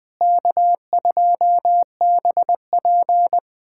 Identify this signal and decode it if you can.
Morse code, heard as a keyed tone, about 21 wpm, 700 Hz. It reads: K2BP
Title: USA Callsign DE Callsign - 4 Characters - 20wpm